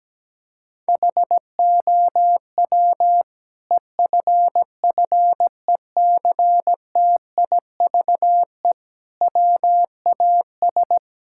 Morse code, heard as a keyed tone, about 17 wpm, 700 Hz. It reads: HOW EFFECTIVE WAS
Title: Sentences from Top 1000 Words - Review Part 6 - 17wpm